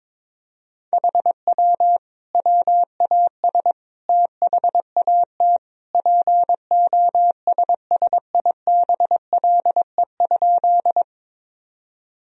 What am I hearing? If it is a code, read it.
Morse code, heard as a keyed tone, about 22 wpm, 700 Hz. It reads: HW WAS THAT POSSIBLE?
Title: Sentences from Top 500 Words and Abbreviations 22wpm